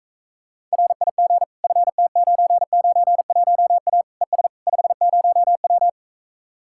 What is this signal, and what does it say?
Morse code, heard as a keyed tone, about 45 wpm, 700 Hz. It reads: RIG FT991A ES 50W